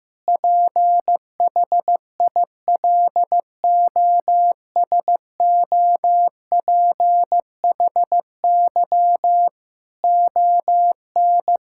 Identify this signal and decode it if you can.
Morse code, heard as a keyed tone, about 15 wpm, 700 Hz. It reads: PHILOSOPHY ON
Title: Sentences from Top 1000 Words - Review Part 6 - 15wpm